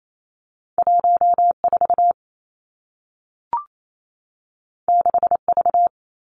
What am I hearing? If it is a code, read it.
Morse code, heard as a keyed tone, about 28 wpm, 700 Hz. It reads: 1 4 6 V
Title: Two Challenging Letter-Number Combinations 28wpm